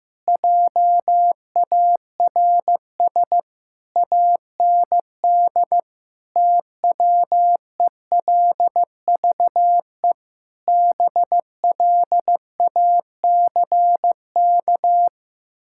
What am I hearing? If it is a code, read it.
Morse code, heard as a keyed tone, about 15 wpm, 700 Hz. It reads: JARS AND TWELVE BLACK
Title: Pangram Challenge v2 - 15wpm